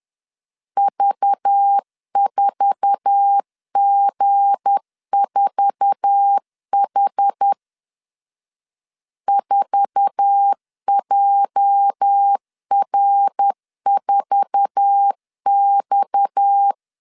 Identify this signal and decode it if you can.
Morse code, heard as a keyed tone, about 10 wpm, 800 Hz. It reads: V4G4H 4JR4X
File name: LEZIONI / 8 YJ45 / 8 YJ4 10wpm 2633